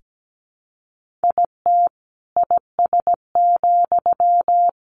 Morse code, heard as a keyed tone, about 17 wpm, 700 Hz. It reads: IT IS,
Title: Sentences from Top 1400 Words 17wpm